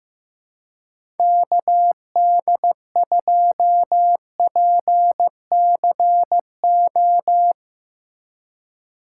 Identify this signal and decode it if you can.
Morse code, heard as a keyed tone, about 15 wpm, 700 Hz. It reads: KD2PCO
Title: USA Call Signs - 6 Characters - 3x Character Spacing - 15wpm